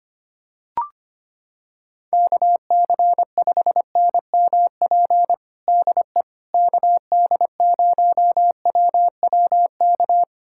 Morse code, heard as a keyed tone, about 25 wpm, 700 Hz. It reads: KC5NMP DE KD0WWK
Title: USA Callsign DE Callsign 25wpm